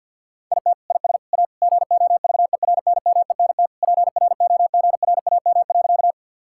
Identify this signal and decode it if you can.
Morse code, heard as a keyed tone, about 47 wpm, 700 Hz. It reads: IT IS A GOVERNMENT PROGRAM.
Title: Sentences from Top 1000 Words - Review Part 5 - 50wpm